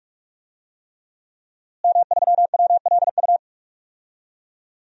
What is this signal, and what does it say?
Morse code, heard as a keyed tone, about 47 wpm, 700 Hz. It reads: M3WLU
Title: DXCC-100 Call Signs - 5 Characters - Repeated 3x - 45wpm